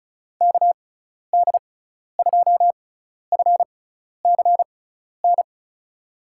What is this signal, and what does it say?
Morse code, heard as a keyed tone, about 36 wpm, spaced as for 12 wpm, 700 Hz. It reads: KD2FCN
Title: USA Call Signs - 6x Character Spacing - 35wpm